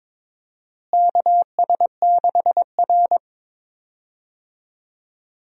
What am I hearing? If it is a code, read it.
Morse code, heard as a keyed tone, about 22 wpm, 700 Hz. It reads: KS6R
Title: USA Call Signs - 4 Characters - Repeated 2x - 22wpm